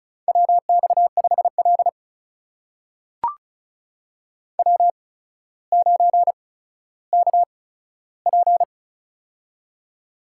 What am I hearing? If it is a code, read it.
Morse code, heard as a keyed tone, about 36 wpm, spaced as for 15 wpm, 700 Hz. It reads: WX5L W9KP
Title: USA Call Signs - 4 Characters - 8x Character Spacing - 35wpm